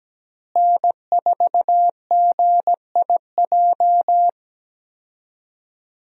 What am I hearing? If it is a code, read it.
Morse code, heard as a keyed tone, about 17 wpm, 700 Hz. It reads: N4GIJ
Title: USA Call Signs - 5 Characters - Repeated 2x - 17wpm